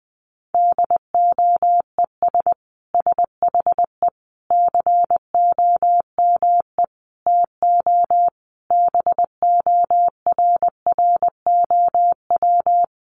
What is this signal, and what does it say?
Morse code, heard as a keyed tone, about 20 wpm, 700 Hz. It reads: DOES SHE COME TO BORROW
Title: Sentences from Top 1500 Words 20wpm